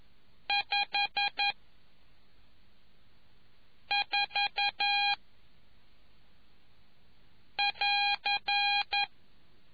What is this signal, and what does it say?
Morse code, heard as a keyed tone, about 11 wpm, 2450 Hz. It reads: 5 4 <AR>